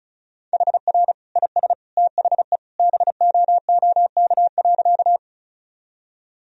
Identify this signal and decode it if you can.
Morse code, heard as a keyed tone, about 36 wpm, 700 Hz. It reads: HR IS THE BOOK.
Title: Sentences from Top 300 Words and Abbreviations 35wpm